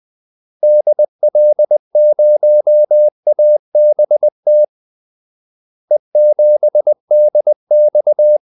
Morse code, heard as a keyed tone, about 20 wpm, 600 Hz. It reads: DL0ABT E7DX
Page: Kaldesignaler 20 wpm